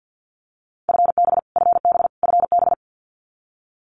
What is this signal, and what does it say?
Morse code, heard as a keyed tone, about 48 wpm, 700 Hz. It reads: FB FB FB
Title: Abbreviations Repeated 3x - 50wpm